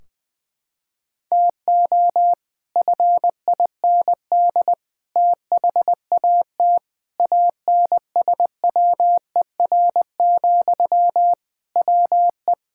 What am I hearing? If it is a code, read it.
Morse code, heard as a keyed tone, about 21 wpm, 700 Hz. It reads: TO FIND THAT ANSWER, WE
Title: Sentences from Top 1600 Words 20wpm